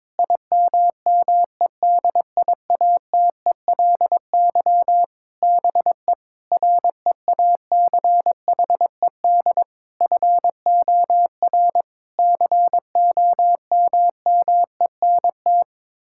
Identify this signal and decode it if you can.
Morse code, heard as a keyed tone, about 22 wpm, 700 Hz. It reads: IMMEDIATELY BE REACHED FOR COMMENT
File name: practice / 22wpm / Sets of 7 Words - 22wpm